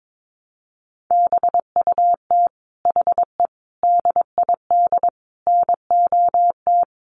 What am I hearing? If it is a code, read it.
Morse code, heard as a keyed tone, about 22 wpm, 700 Hz. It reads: BUT HE DID NOT
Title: Sentences from Top 600 Words 22wpm